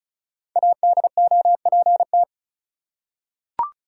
Morse code, heard as a keyed tone, about 34 wpm, 700 Hz. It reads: ADOPT
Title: Five Letter Words - 35wpm